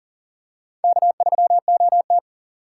Morse code, heard as a keyed tone, about 40 wpm, 700 Hz. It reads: K3OT
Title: USA Call Signs - 4 Characters - 2x Character Spacing - 40wpm